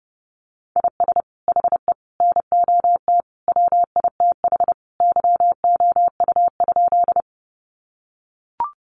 Morse code, heard as a keyed tone, about 30 wpm, 700 Hz. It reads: IS HE NOT WITH YOU?
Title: Sentences from Top 600 Words 30wpm